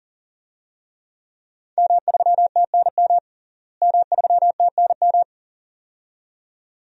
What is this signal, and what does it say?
Morse code, heard as a keyed tone, about 40 wpm, 700 Hz. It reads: M3TNM M3TNM
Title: DXCC-100 Call Signs - 5 Characters - Repeated 2x - 40wpm